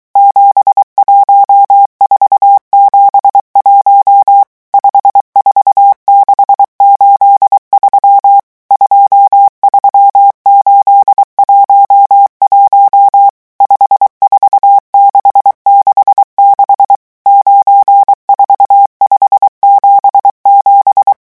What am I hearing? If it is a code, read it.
Morse code, heard as a keyed tone, about 23 wpm, 800 Hz. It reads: 71471 54683 23811 54666 94577
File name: morse mp3 / revision11 chiffres vitesse 22